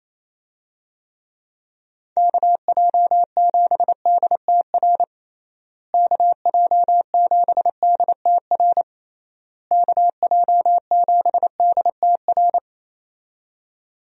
Morse code, heard as a keyed tone, about 28 wpm, 700 Hz. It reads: KJ7DTR KJ7DTR KJ7DTR
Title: USA Call Signs - 6 Characters - Repeated 3x - 28wpm